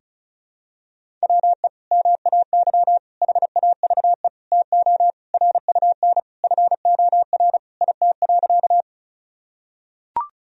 Morse code, heard as a keyed tone, about 36 wpm, 700 Hz. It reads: WE MAY HAVE TO RUN FOR IT.
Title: Sentences from Top 1000 Words - Review Part 1 - 35wpm